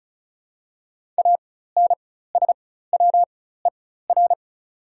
Morse code, heard as a keyed tone, about 34 wpm, spaced as for 16 wpm, 700 Hz. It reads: ANSWER
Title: 4x Character Spacing - Top 500 Words - 35wpm